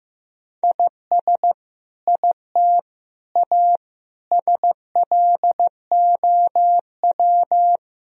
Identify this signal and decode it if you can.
Morse code, heard as a keyed tone, about 15 wpm, 700 Hz. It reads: IS IT A SLOW
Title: Sentences from Top 1000 Words - Review Part 2 - 15wpm